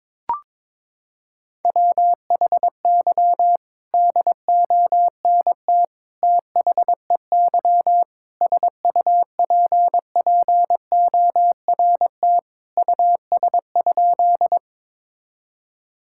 Morse code, heard as a keyed tone, about 22 wpm, 700 Hz. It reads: WHY DONT THEY SUPPORT US?
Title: Sentences from Top 1000 Words - Review Part 5 - 22wpm